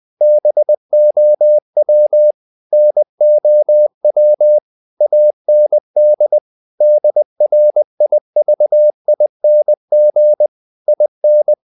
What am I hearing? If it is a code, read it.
Morse code, heard as a keyed tone, about 20 wpm, 600 Hz. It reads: BOW NOW AND DRIVING IN
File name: the old man and the sea part 2 20wpm